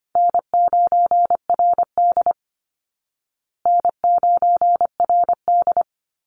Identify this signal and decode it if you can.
Morse code, heard as a keyed tone, about 24 wpm, 700 Hz. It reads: N9RD N9RD
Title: Easy Call Signs Repeated 2x - 25wpm